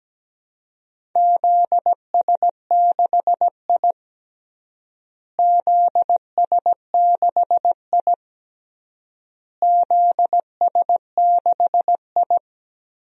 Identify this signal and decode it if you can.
Morse code, heard as a keyed tone, about 17 wpm, 700 Hz. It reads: ZS6I ZS6I ZS6I
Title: DXCC-100 Call Signs - 4 Characters - Repeated 3x - 17wpm